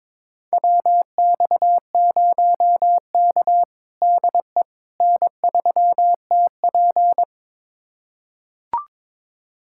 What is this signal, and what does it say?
Morse code, heard as a keyed tone, about 22 wpm, 700 Hz. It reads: WX0K DE N3TP
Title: USA Callsign DE Callsign - 4 Characters - 22wpm